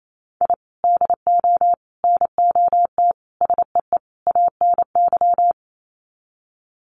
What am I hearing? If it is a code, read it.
Morse code, heard as a keyed tone, about 28 wpm, 700 Hz. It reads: I DO NOT SEE ANY
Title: Sets of 5 Words 28wpm